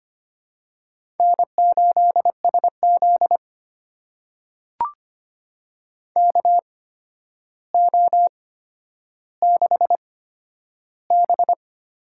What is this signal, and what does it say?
Morse code, heard as a keyed tone, about 25 wpm, spaced as for 11 wpm, 700 Hz. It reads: N8SZ KO6B
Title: USA Call Signs - 4 Characters - 8x Character Spacing - 25wpm